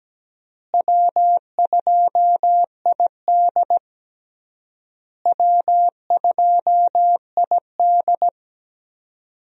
Morse code, heard as a keyed tone, about 17 wpm, 700 Hz. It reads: W2ID W2ID
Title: USA Call Signs - 4 Characters - Repeated 3x - 17wpm